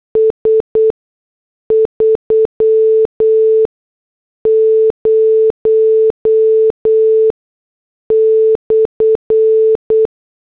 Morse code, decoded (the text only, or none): S30/